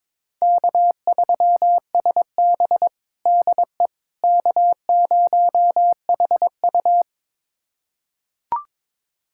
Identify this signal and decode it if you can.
Morse code, heard as a keyed tone, about 22 wpm, 700 Hz. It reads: K3SB DE K0HU
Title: USA Callsign DE Callsign - 4 Characters - 22wpm